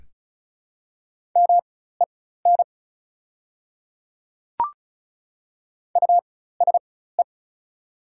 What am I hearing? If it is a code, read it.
Morse code, heard as a keyed tone, about 34 wpm, spaced as for 16 wpm, 700 Hz. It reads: MEN USE
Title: 4x Character Spacing - Top 200 Words - 35wpm